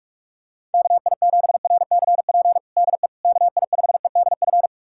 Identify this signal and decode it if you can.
Morse code, heard as a keyed tone, about 46 wpm, 700 Hz. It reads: KI7RCP DE KI5EDF
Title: USA Callsign DE Callsign 45wpm